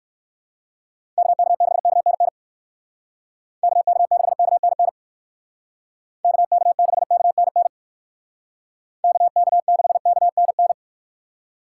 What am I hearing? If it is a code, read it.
Morse code, heard as a keyed tone, about 67 wpm, 700 Hz. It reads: KK6KNN KK6KNN KK6KNN KK6KNN